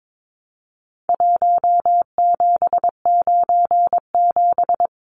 Morse code, heard as a keyed tone, about 22 wpm, 700 Hz. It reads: 1797
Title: Numbers - Contest Style - 4-Digit 22wpm